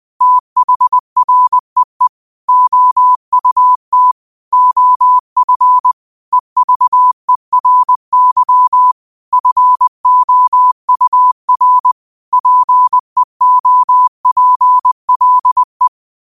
Morse code, heard as a keyed tone, about 20 wpm, 1000 Hz. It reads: THREE OUT OF EVERY FOUR PEOPLE